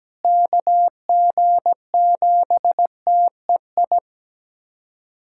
Morse code, heard as a keyed tone, about 17 wpm, 700 Hz. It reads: KG7TEI
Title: USA Callsign DE Callsign - 6 Characters - 17wpm